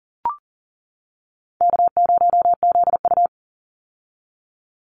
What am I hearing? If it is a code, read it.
Morse code, heard as a keyed tone, about 40 wpm, 700 Hz. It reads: K0ZU
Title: Easy Call Signs - 40wpm